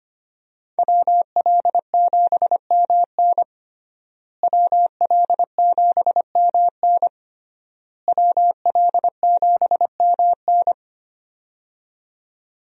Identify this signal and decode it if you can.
Morse code, heard as a keyed tone, about 25 wpm, 700 Hz. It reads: WL7MN WL7MN WL7MN
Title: USA Call Signs - 5 Characters - Repeated 3x - 25wpm